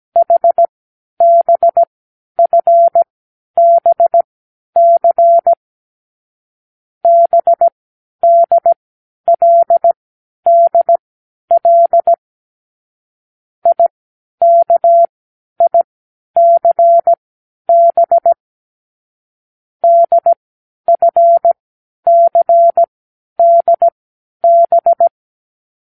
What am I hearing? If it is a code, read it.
Morse code, heard as a keyed tone, about 17 wpm, spaced as for 10 wpm, 680 Hz. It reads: HBFBC BDLDL IKICB DFCDB
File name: k7qo / Lesson18(A-L)